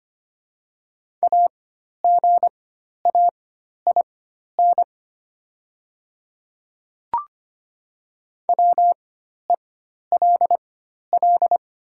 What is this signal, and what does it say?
Morse code, heard as a keyed tone, about 25 wpm, spaced as for 12 wpm, 700 Hz. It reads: AGAIN WELL
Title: 4x Character Spacing - Top 200 Words - 25wpm